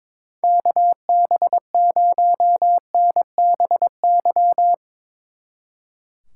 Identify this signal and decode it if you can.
Morse code, heard as a keyed tone, about 22 wpm, 700 Hz. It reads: KB0NBY
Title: Speed Racing - USA 2x3 Call Signs 22wpm